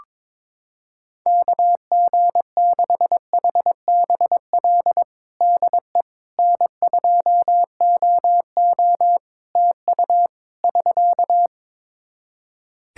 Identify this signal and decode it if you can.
Morse code, heard as a keyed tone, about 22 wpm, 700 Hz. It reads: KG6HBL DE N2OO TU <SK>
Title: K1USN SST QSO Practice 22wpm